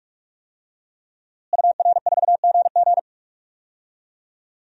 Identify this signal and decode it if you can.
Morse code, heard as a keyed tone, about 46 wpm, 700 Hz. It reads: UR4GG
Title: DXCC-100 Call Signs - 5 Characters - Repeated 3x - 45wpm